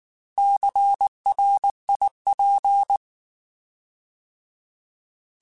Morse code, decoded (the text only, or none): CRIP